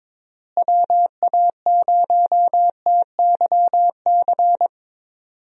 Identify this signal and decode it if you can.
Morse code, heard as a keyed tone, about 22 wpm, 700 Hz. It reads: WA0TYC
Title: USA Call Signs - 6 Characters - Repeated 3x - 22wpm